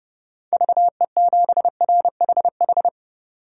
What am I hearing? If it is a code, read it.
Morse code, heard as a keyed tone, about 30 wpm, 700 Hz. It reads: VE7RHH
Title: DXCC-100 Call Signs - 6 Characters - 30wpm